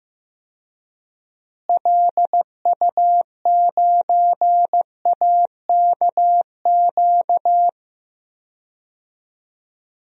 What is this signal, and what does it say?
Morse code, heard as a keyed tone, about 15 wpm, 700 Hz. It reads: LU9AKQ
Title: DXCC-100 Call Signs - 6 Characters - Repeated 3x - 15wpm